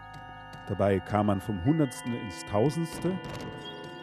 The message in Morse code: none

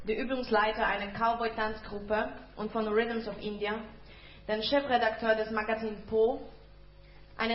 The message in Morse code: none

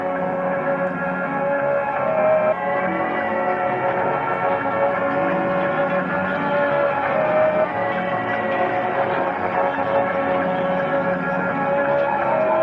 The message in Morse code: none

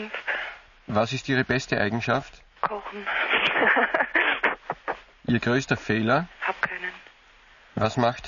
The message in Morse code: none

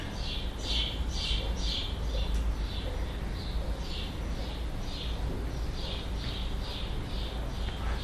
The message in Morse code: none

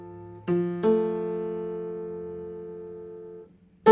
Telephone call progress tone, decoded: dial tone

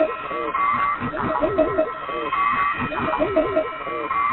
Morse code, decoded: none

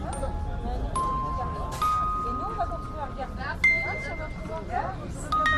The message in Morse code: none

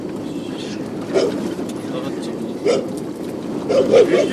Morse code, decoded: none

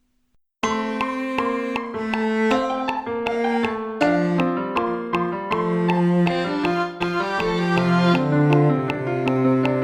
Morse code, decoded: none

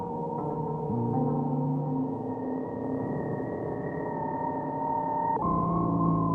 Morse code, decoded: none

